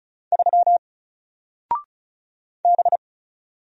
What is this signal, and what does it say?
Morse code, heard as a keyed tone, about 36 wpm, 700 Hz. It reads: 3 B
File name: Single-Letter-Number-35wpm